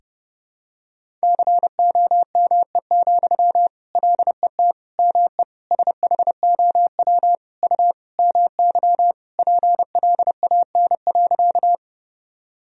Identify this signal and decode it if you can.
Morse code, heard as a keyed tone, about 31 wpm, 700 Hz. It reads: COME, LET ME SHOW U MY PLAN.
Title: Sentences from Top 500 Words and Abbreviations 30wpm